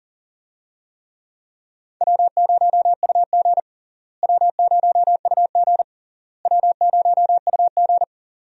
Morse code, heard as a keyed tone, about 40 wpm, 700 Hz. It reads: W0UG W0UG W0UG
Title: USA Call Signs - 4 Characters - Repeated 3x - 40wpm